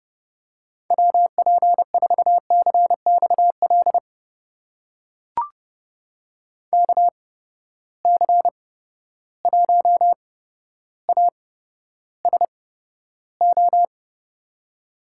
Morse code, heard as a keyed tone, about 30 wpm, spaced as for 13 wpm, 700 Hz. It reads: WP4CXL KC1ASO
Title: USA Call Signs - 6 Characters - 8x Character Spacing - 30wpm